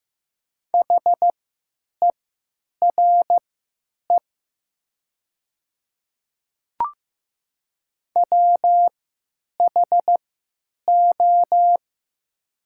Morse code, heard as a keyed tone, about 15 wpm, spaced as for 9 wpm, 700 Hz. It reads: HERE WHO
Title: 3x Character Spacing - Top 200 Words - 15wpm